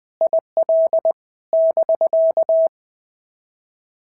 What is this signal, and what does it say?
Morse code, heard as a keyed tone, about 20 wpm, 650 Hz. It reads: IL <BK>
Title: POTA QSO Practice - 20wpm